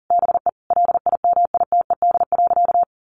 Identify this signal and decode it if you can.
Morse code, heard as a keyed tone, about 37 wpm, 700 Hz. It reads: BE LIMITED.
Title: Sentences from Top 900 Words 40wpm